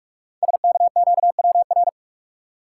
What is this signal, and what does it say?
Morse code, heard as a keyed tone, about 48 wpm, 700 Hz. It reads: SKQWR
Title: Code Groups - 5 Letter:Numbers - 45wpm